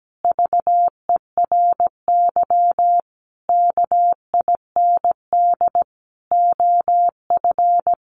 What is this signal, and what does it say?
Morse code, heard as a keyed tone, about 17 wpm, 700 Hz. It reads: VERY KIND OF
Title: Sentences from Top 300 Words 17wpm